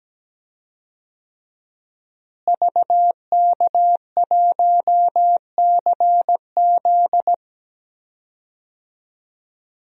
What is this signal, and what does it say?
Morse code, heard as a keyed tone, about 17 wpm, 700 Hz. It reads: VK1CZ
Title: DXCC-100 Call Signs - 5 Characters - Repeated 3x - 17wpm